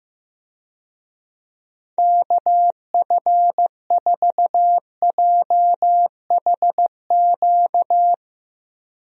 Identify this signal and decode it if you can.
Morse code, heard as a keyed tone, about 16 wpm, 700 Hz. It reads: KF4JHQ
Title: USA Call Signs - 6 Characters - Repeated 3x - 15wpm